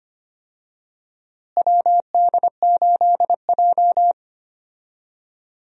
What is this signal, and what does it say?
Morse code, heard as a keyed tone, about 25 wpm, 700 Hz. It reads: WD8J
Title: USA Call Signs - 4 Characters - Repeated 3x - 25wpm